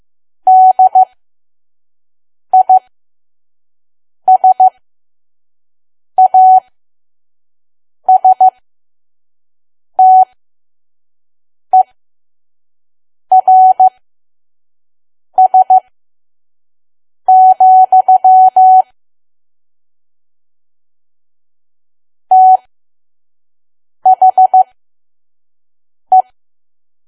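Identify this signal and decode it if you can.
Morse code, heard as a keyed tone, about 15 wpm, spaced as for 5 wpm, 750 Hz. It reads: DISASTERS, THE
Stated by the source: Morse code practice at 5 WPM from KQ2P.COM.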